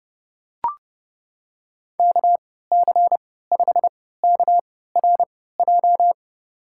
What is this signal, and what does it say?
Morse code, heard as a keyed tone, about 30 wpm, spaced as for 17 wpm, 700 Hz. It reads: KC5KRJ
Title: USA Call Signs - 6 Characters - 3x Character Spacing - 30wpm